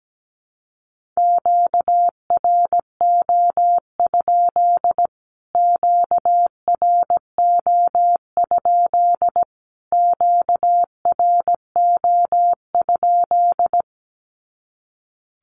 Morse code, heard as a keyed tone, about 17 wpm, 700 Hz. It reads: QRO? QRO? QRO?
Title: Q-Codes Repeated 3x 17wpm